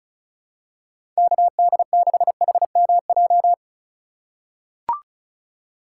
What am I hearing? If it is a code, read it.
Morse code, heard as a keyed tone, about 36 wpm, 700 Hz. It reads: KD6HMJ
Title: USA Call Signs - 6 Characters - 3x Character Spacing - 35wpm